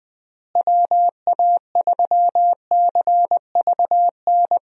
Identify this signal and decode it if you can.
Morse code, heard as a keyed tone, about 21 wpm, 700 Hz. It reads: WA3CVN
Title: USA Call Signs - 6 Characters - Repeated 3x - 20wpm